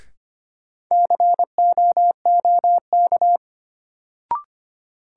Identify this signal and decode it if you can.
Morse code, heard as a keyed tone, about 25 wpm, 700 Hz. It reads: COOK
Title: Sentences from Top 700 Words - New Word Drill - 25wpm